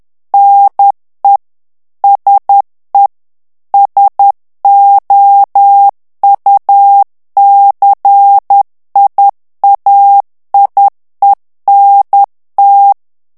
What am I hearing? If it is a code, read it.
Morse code, heard as a keyed tone, about 10 wpm, 800 Hz. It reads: NE SE SOUCIAIENT
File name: morse mp3 / FRANCAIS TEXTE CLAIR 2 VITESSE 10 MOTS MINUTES